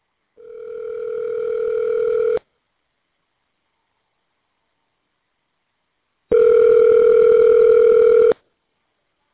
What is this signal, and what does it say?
Ringback tone (North America), 440 + 480 Hz.